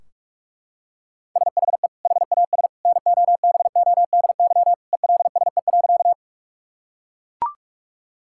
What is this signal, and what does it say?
Morse code, heard as a keyed tone, about 41 wpm, 700 Hz. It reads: SHE HAS NOBODY ELSE.
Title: Sentences from Top 1600 Words 45wpm